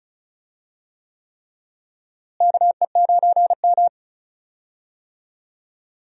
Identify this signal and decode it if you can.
Morse code, heard as a keyed tone, about 34 wpm, 700 Hz. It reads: KE9M
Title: USA Call Signs - 4 Characters - Repeated 3x - 35wpm